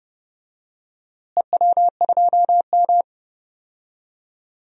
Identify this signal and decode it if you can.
Morse code, heard as a keyed tone, about 30 wpm, 700 Hz. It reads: EW2M
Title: DXCC-100 Call Signs - 4 Characters - Repeated 2x - 30wpm